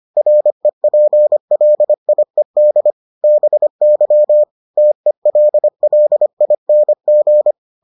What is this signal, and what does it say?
Morse code, heard as a keyed tone, about 25 wpm, 600 Hz. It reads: REPLIED BY TELLING